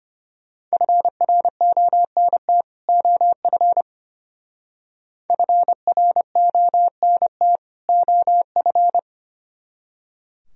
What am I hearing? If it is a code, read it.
Morse code, heard as a keyed tone, about 31 wpm, 700 Hz. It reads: FRONT OF FRONT OF
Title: Speed-Racing - Sets of 2 Words 25wpm